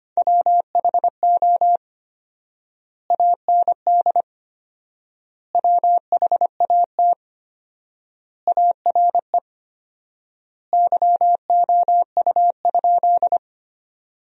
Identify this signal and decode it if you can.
Morse code, heard as a keyed tone, about 25 wpm, 700 Hz. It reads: WHO AND WHAT ARE YOU?
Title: 4x Word-Spacing - Sentences from Top 100 Words - 25wpm